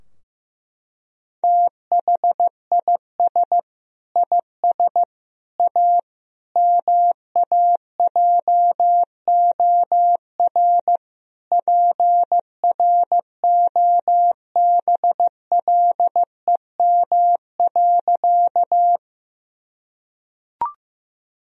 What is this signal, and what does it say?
Morse code, heard as a keyed tone, about 15 wpm, 700 Hz. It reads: THIS IS A MAJOR PROBLEM.
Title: Sentences from Top 1000 Words - Review Part 4 - 15wpm